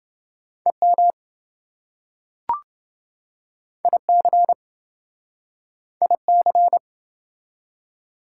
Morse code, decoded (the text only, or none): EM IC IC